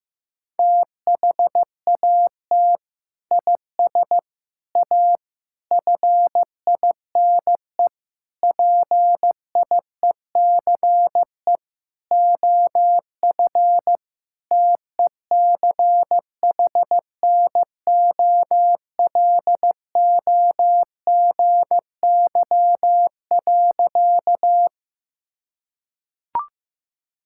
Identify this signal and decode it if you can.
Morse code, heard as a keyed tone, about 15 wpm, 700 Hz. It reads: THAT IS A FINE PIECE OF TECHNOLOGY.
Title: Sentences from Top 1000 Words - Review Part 5 - 15wpm